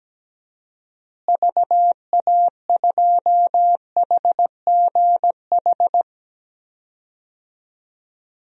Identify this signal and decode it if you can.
Morse code, heard as a keyed tone, about 17 wpm, 700 Hz. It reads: VA2HGH
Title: DXCC-100 Call Signs - 6 Characters - Repeated 2x - 17wpm